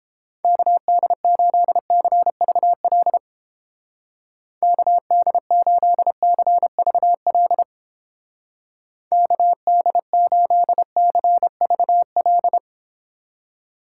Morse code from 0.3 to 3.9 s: KD8CVL

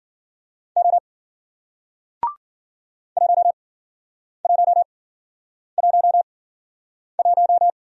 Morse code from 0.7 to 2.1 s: O

